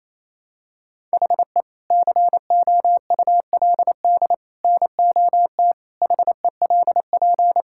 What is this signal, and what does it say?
Morse code, heard as a keyed tone, about 28 wpm, 700 Hz. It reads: HE COULD NOT HELP
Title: Sentences from Top 1600 Words 28wpm